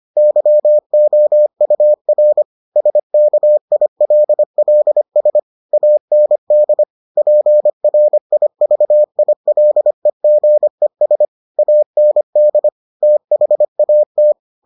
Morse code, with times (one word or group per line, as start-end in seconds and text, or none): 0.2-2.4 YOUR
2.8-5.4 SKILLS
5.7-6.8 AND
7.2-11.2 PRIVILEGES
11.6-12.7 AND
13.0-14.3 THAT